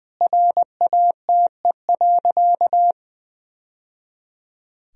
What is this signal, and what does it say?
Morse code, heard as a keyed tone, about 20 wpm, 700 Hz. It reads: RATE.
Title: Sentences from Top 1000 Words - Review Part 4 - 20wpm